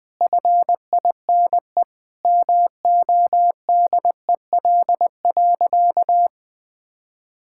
Morse code, heard as a keyed tone, about 21 wpm, 700 Hz. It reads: FINE MODEL.
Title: Sentences from Top 1000 Words - Review Part 5 - 20wpm